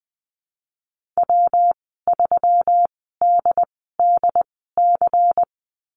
Morse code, heard as a keyed tone, about 20 wpm, spaced as for 15 wpm, 700 Hz. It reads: W3DDC
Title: 173 Lesson 28g - Number 9 - Call Signs 20wpm